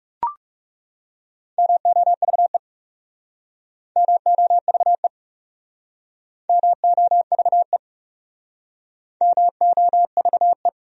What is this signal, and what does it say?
Morse code, heard as a keyed tone, about 40 wpm, 700 Hz. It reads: MOVE MOVE MOVE MOVE